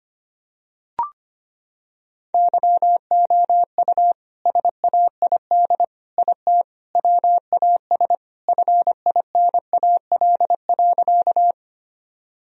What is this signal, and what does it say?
Morse code, heard as a keyed tone, about 25 wpm, 700 Hz. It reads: YOU SAID IT WAS FINAL.
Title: Sentences from Top 1000 Words - Review Part 2 - 25wpm